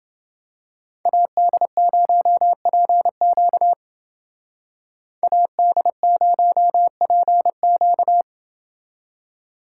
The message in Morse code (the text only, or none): AD0PQ AD0PQ